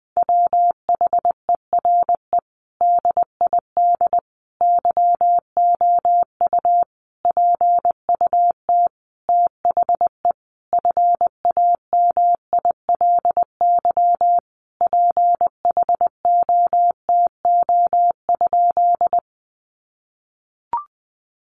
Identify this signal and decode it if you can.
Morse code, heard as a keyed tone, about 20 wpm, 700 Hz. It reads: WHERE DID YOU PUT THE FAMILY PHOTO?
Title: Sentences from Top 1200 Words 20wpm